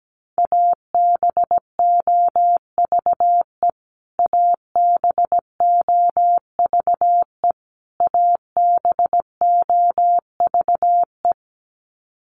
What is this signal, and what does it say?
Morse code, heard as a keyed tone, about 17 wpm, 700 Hz. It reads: ABOVE ABOVE ABOVE
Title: Top 500 Words Repeated 17wpm